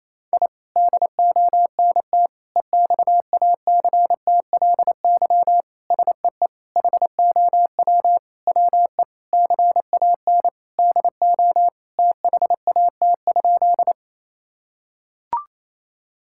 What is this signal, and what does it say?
Morse code, heard as a keyed tone, about 27 wpm, 700 Hz. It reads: I DONT EXACTLY SEE HOW WE CAN DO THAT?
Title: Sentences from Top 1000 Words - Review Part 6 - 28wpm